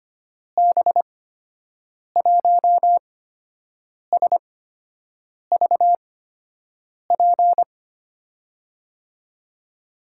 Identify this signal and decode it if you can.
Morse code, heard as a keyed tone, about 25 wpm, spaced as for 7 wpm, 700 Hz. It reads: B1SVP
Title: USA Call Signs - 6 Characters - 8x Character Spacing - 25wpm